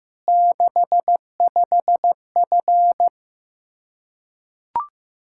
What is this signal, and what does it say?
Morse code, heard as a keyed tone, about 16 wpm, 700 Hz. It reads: 65F
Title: Practice QSO - Exchange 2 - Info - 15wpm